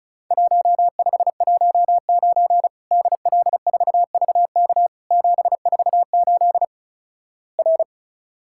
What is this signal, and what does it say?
Morse code, heard as a keyed tone, about 34 wpm, 700 Hz. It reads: 1519 DL4VK 748 R
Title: Worked All Europe DX Contest - QTC Exchange - 35wpm